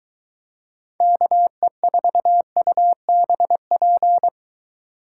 Morse code, heard as a keyed tone, about 23 wpm, 700 Hz. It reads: KE4UBP